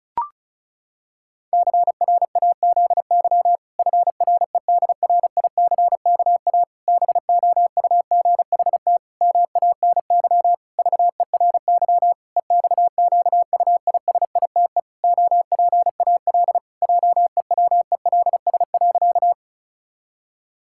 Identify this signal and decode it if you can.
Morse code, heard as a keyed tone, about 34 wpm, 700 Hz. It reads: CRAZY FREDRICKA BOUGHT MANY VERY EXQUISITE OPAL JEWELS.
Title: Pangram Challenge v2 - 35wpm